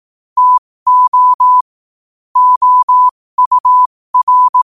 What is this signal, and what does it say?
Morse code, heard as a keyed tone, about 18 wpm, spaced as for 15 wpm, 1000 Hz. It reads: TO OUR
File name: qotdfiles / 15wpm